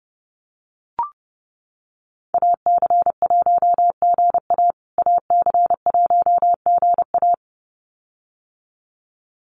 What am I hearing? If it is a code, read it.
Morse code, heard as a keyed tone, about 30 wpm, 700 Hz. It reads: AC1GA AC1GA
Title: USA Call Signs Repeated 2x 30wpm